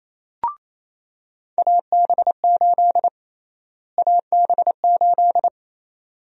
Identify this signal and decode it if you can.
Morse code, heard as a keyed tone, about 28 wpm, 700 Hz. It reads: AB8 AB8
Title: USA Call Sign Prefixes - Repeated 2x - 28wpm